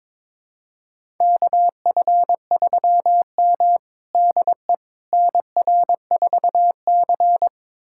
Morse code, heard as a keyed tone, about 22 wpm, 700 Hz. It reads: KF3M DE NR4C
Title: USA Callsign DE Callsign - 4 Characters - 22wpm